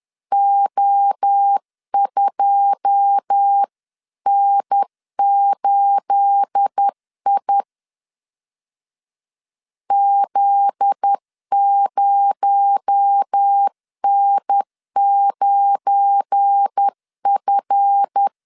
Morse code, heard as a keyed tone, about 10 wpm, 800 Hz. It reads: O2N8I Z0N9F
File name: LEZIONI / 9 286 10wpm 2735